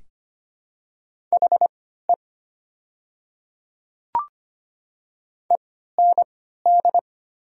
Morse code, heard as a keyed tone, about 25 wpm, spaced as for 14 wpm, 700 Hz. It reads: HE END
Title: 3x Character Spacing - Top 500 Words - 25wpm